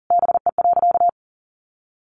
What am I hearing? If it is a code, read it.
Morse code, heard as a keyed tone, about 40 wpm, 700 Hz. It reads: BE.
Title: Sentences from Top 700 Words 40wpm